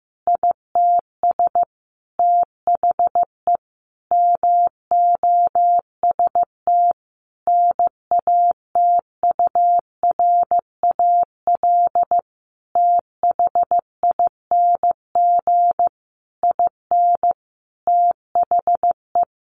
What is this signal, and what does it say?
Morse code, heard as a keyed tone, about 15 wpm, 700 Hz. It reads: ITS THE MOST NATURAL THING IN THE
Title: Sentences from Top 700 Words 15wpm